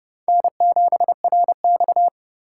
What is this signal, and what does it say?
Morse code, heard as a keyed tone, about 35 wpm, 700 Hz. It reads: N7RX
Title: Speed Racing - USA 1x2 Call Signs 30wpm